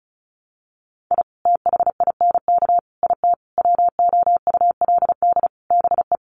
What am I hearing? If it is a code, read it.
Morse code, heard as a keyed tone, about 34 wpm, 700 Hz. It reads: I THINK IT WOULD BE
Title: Sets of 5 Words 35wpm